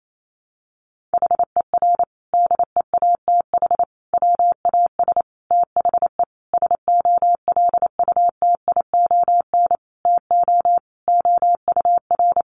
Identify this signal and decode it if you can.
Morse code, heard as a keyed tone, about 28 wpm, 700 Hz. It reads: HER DEATH WAS THE SOLUTION TO OUR
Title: Sentences from Top 1300 Words 28wpm